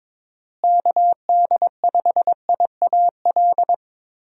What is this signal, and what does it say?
Morse code, heard as a keyed tone, about 22 wpm, 700 Hz. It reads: KD5IAL
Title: USA Call Signs - 6 Characters - Repeated 2x - 22wpm